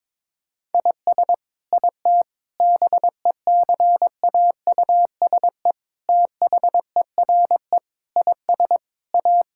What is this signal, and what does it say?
Morse code, heard as a keyed tone, about 21 wpm, 700 Hz. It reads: IS IT BECAUSE THERE IS A
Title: Sentences from Top 1000 Words - Review Part 5 - 22wpm